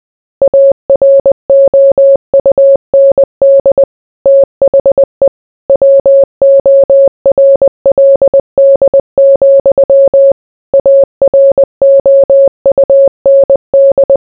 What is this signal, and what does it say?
Morse code, heard as a keyed tone, about 20 wpm, 550 Hz. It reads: AROUND THE WORLD, AROUND